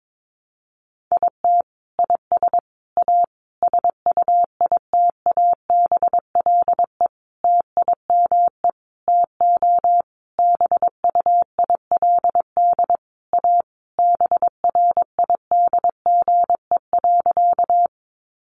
Morse code, heard as a keyed tone, about 22 wpm, 700 Hz. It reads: IT IS A SUITABLE TIME TO BUILD A BRIDGE.
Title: Sentences from Top 1400 Words 22wpm